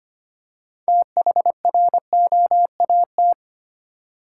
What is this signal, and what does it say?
Morse code, heard as a keyed tone, about 25 wpm, 700 Hz. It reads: THROAT
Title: Six Letter Words - 25wpm